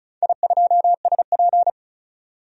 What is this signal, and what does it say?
Morse code, heard as a keyed tone, about 36 wpm, 700 Hz. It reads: I2SP
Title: DXCC-100 Call Signs - 4 Characters - 35wpm